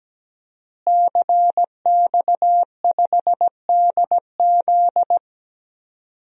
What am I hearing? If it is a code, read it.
Morse code, heard as a keyed tone, about 17 wpm, 700 Hz. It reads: CX5DZ
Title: DXCC-100 Call Signs - 5 Characters - Repeated 3x - 17wpm